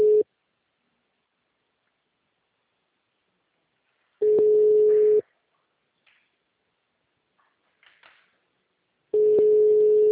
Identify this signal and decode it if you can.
Ringback tone (Europe, 425 Hz).
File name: out- -2014- - - .9228